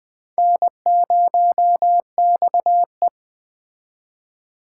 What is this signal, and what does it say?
Morse code, heard as a keyed tone, about 21 wpm, 700 Hz. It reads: N0XE
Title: Speed Racing - USA 1x2 Call Signs 17wpm